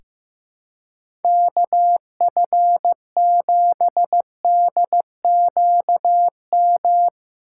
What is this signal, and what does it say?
Morse code, heard as a keyed tone, about 15 wpm, 700 Hz. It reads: KF7DQM
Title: USA Call Signs - 6 Characters - 2x Character Spacing - 15wpm